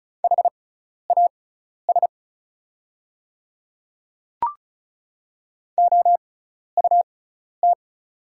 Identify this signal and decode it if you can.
Morse code, heard as a keyed tone, about 34 wpm, spaced as for 12 wpm, 700 Hz. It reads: HAS OUT
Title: 6x Character Spacing - Top 100 Words - 35wpm